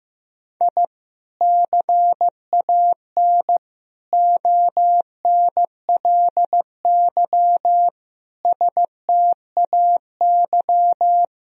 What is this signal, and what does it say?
Morse code, heard as a keyed tone, about 16 wpm, 700 Hz. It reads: I CAN ONLY STAY
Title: Sentences from Top 1000 Words - Review Part 3 - 15wpm